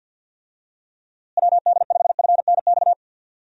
Morse code, heard as a keyed tone, about 50 wpm, 700 Hz. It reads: WD5FNX
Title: USA Call Signs - 6 Characters - Repeated 2x - 50wpm